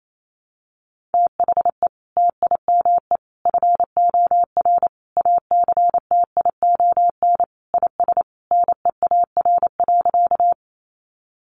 Morse code, heard as a keyed tone, about 28 wpm, 700 Hz. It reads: THE TIME FOR ACTION IS NEAR.
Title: Sentences from Top 700 Words 28wpm